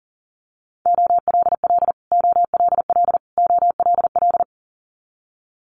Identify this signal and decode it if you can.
Morse code, heard as a keyed tone, about 40 wpm, 700 Hz. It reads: OLL OLL OLL
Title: Multi-letter Phonemes Repeated 3x 40wpm